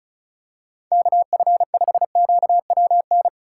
Morse code, heard as a keyed tone, about 34 wpm, 700 Hz. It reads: KF5QWN
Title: USA Call Signs - 6 Characters - 8x Character Spacing - 35wpm